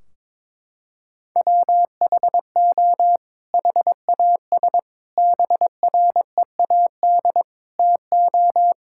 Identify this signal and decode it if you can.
Morse code, heard as a keyed tone, about 22 wpm, 700 Hz. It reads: WHO HAS BREAD TO
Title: Sentences from Top 1600 Words 22wpm